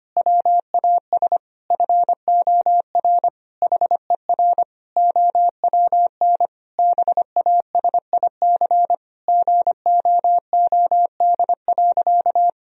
Morse code, heard as a keyed tone, about 25 wpm, 700 Hz. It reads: WAS FOR HER OWN BASIC GOOD.
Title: Sentences from Top 1000 Words - Review Part 3 - 25wpm